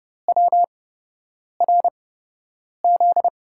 Morse code, heard as a keyed tone, about 30 wpm, spaced as for 8 wpm, 700 Hz. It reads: WRZ
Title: USA Call Signs - 5 Characters - 8x Character Spacing - 30wpm